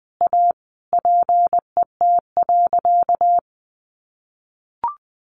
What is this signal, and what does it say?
Morse code, heard as a keyed tone, about 20 wpm, 700 Hz. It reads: A PET.
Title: Additional Easy Phrases 20wpm